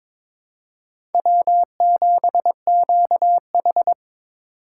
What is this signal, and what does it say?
Morse code, heard as a keyed tone, about 22 wpm, 700 Hz. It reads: W7QH
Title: USA Call Signs - 3x Character Spacing - 22wpm